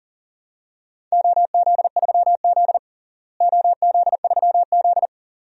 Morse code, heard as a keyed tone, about 40 wpm, 690 Hz. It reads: OZ3Z OZ3Z
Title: DXCC-100 Call Signs - 4 Characters - Repeated 2x - 40wpm